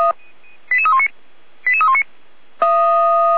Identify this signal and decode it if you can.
Five-tone selective call (ZVEI), heard as quick runs of five tones, tones of about 70 ms. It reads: 88319 88319